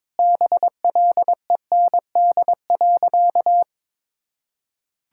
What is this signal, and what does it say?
Morse code, heard as a keyed tone, about 22 wpm, 700 Hz. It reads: BLEND.
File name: practice / 22wpm / Sentences with 4-Word Phrases - 22wpm